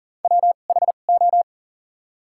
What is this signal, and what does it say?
Morse code, heard as a keyed tone, about 40 wpm, spaced as for 29 wpm, 700 Hz. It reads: WHO
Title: 2x Character Spacing - Top 100 Words - 40wpm